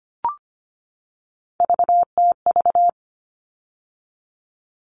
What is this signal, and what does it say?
Morse code, heard as a keyed tone, about 25 wpm, 700 Hz. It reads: VTV
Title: All Cut Numbers - 3-Digit 25wpm